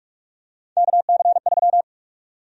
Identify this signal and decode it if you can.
Morse code, heard as a keyed tone, about 41 wpm, 700 Hz. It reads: KC3
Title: USA Call Sign Prefixes - 45wpm